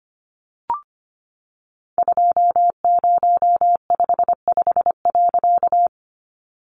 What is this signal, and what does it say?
Morse code, heard as a keyed tone, about 25 wpm, 700 Hz. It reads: 2055.
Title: A Sound Of Thunder 25wpm